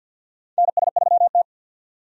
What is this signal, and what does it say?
Morse code, heard as a keyed tone, about 47 wpm, 700 Hz. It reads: NS3T
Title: USA Call Signs - 4 Characters - 50wpm